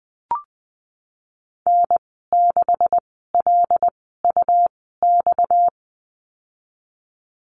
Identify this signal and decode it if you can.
Morse code, heard as a keyed tone, about 20 wpm, 700 Hz. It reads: N 6 L U X
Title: 238 Lesson 38g - Letter q - Call Signs 20wpm